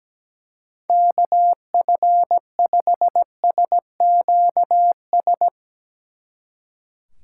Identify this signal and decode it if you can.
Morse code, heard as a keyed tone, about 18 wpm, 700 Hz. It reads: KF5SQS
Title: Speed Racing - USA 2x3 Call Signs 17wpm